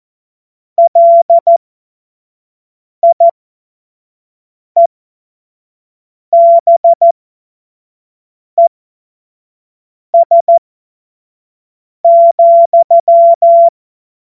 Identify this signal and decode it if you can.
Morse code, heard as a keyed tone, about 14 wpm, spaced as for 5 wpm, 680 Hz. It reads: LIEBES,
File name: Morsekurs / Witze / Witz 06. Dezember 2022 25 F70